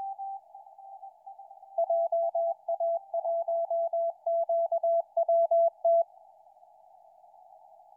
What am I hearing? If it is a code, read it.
Morse code, heard as a keyed tone, about 20 wpm, 670 Hz. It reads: JA1QWT
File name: oe5ke 28-11-13 2007 160m